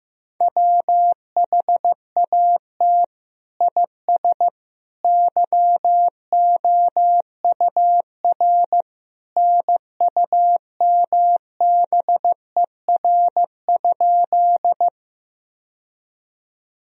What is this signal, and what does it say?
Morse code, heard as a keyed tone, about 15 wpm, 700 Hz. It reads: WHAT IS YOUR NUMBER?
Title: Sentences from Top 1000 Words - Review Part 1 - 15wpm